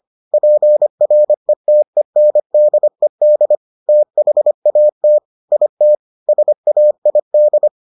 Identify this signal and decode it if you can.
Morse code, heard as a keyed tone, about 25 wpm, 600 Hz. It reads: PRETENDED THAT IT SAID